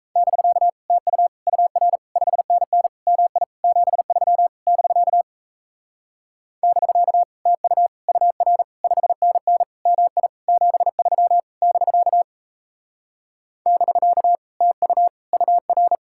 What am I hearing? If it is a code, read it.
Morse code, heard as a keyed tone, about 36 wpm, 700 Hz. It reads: <BK> TU UR 5NN MI 73 <BK> <BK> TU UR 5NN MI 73 <BK> <BK> TU UR